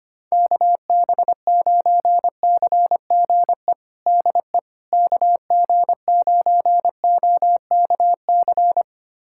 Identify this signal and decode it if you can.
Morse code, heard as a keyed tone, about 25 wpm, 700 Hz. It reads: KB9CGE DE KG9OKC
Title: USA Callsign DE Callsign - 6 Characters - 25wpm